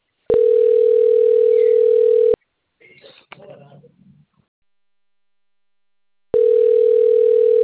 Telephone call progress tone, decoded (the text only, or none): ringback tone